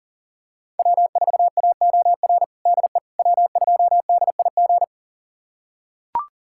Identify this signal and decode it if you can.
Morse code, heard as a keyed tone, about 37 wpm, 700 Hz. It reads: W4AOR DE W2DIG
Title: USA Callsign DE Callsign - 5 Characters - 40wpm